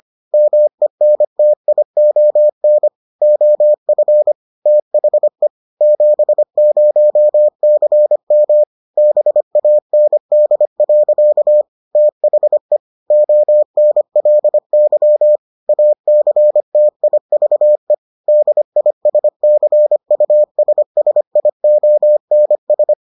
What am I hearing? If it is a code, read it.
Morse code, heard as a keyed tone, about 25 wpm, 600 Hz. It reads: MENTION OF THE 70CM BAND. THE ONLY ACTIVE DISCUSSIONS